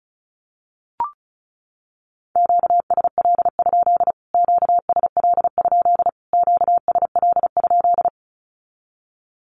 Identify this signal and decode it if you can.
Morse code, heard as a keyed tone, about 34 wpm, 700 Hz. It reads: QSL? QSL? QSL?
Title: Q-Codes Repeated 3x 35wpm